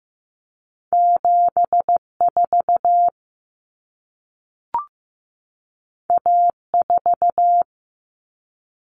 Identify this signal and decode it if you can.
Morse code, heard as a keyed tone, about 15 wpm, 700 Hz. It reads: 74 A4
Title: Common Cut Numbers - 2-Digit 15wpm